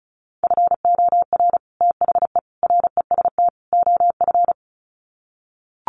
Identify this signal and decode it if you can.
Morse code, heard as a keyed tone, about 34 wpm, 700 Hz. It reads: FOR THE REST OF
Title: Sets of 4 Words 35wpm